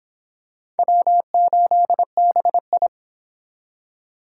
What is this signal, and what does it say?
Morse code, heard as a keyed tone, about 25 wpm, 700 Hz. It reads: W8BI